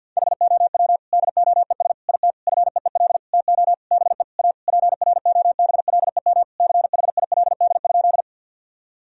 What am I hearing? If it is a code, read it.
Morse code, heard as a keyed tone, about 50 wpm, 700 Hz. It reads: HOW DOES IT FEEL TO BE A PROBLEM CHILD?
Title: Sentences from Top 1000 Words - Review Part 2 - 50wpm